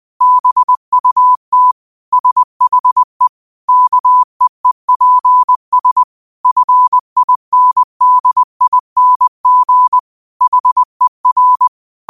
Morse code, heard as a keyed tone, about 20 wpm, 1000 Hz. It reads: BUT SHE KEEPS FINDING HER
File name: qotdfiles / 20wpm